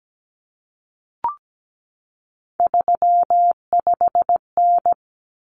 Morse code, heard as a keyed tone, about 17 wpm, 700 Hz. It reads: 35N
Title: Common Cut Numbers - 3-Digit 17wpm